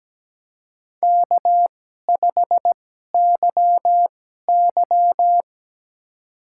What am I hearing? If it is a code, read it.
Morse code, heard as a keyed tone, about 17 wpm, spaced as for 12 wpm, 700 Hz. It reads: K5YY
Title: USA Call Signs - 2x Character Spacing - 17wpm